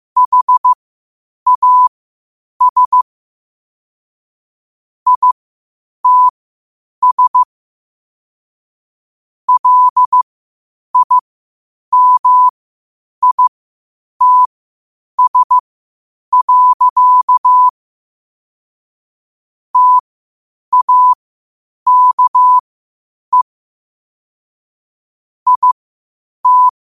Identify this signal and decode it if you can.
Morse code, heard as a keyed tone, about 15 wpm, spaced as for 8 wpm, 1000 Hz. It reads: HAS ITS LIMITS. TAKE IT
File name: qotdfiles / 8wpm